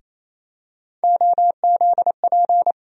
Morse code, heard as a keyed tone, about 28 wpm, 700 Hz. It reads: OZP
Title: USA Call Sign Postfixes - 3 Letters 28wpm